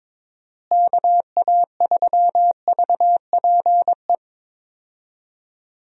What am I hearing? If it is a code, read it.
Morse code, heard as a keyed tone, about 22 wpm, 700 Hz. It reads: KA3VPE
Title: USA Call Signs - 6 Characters - Repeated 3x - 22wpm